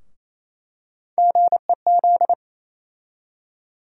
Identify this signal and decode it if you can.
Morse code, heard as a keyed tone, about 28 wpm, 700 Hz. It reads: GEZ
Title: USA Call Sign Postfixes - 3 Letters 28wpm